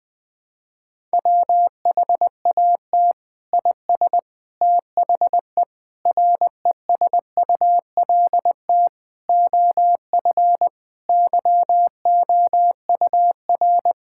Morse code, heard as a keyed tone, about 21 wpm, 700 Hz. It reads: WHAT IS THE RESULT OF YOUR
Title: Sentences from Top 1000 Words - Review Part 5 - 20wpm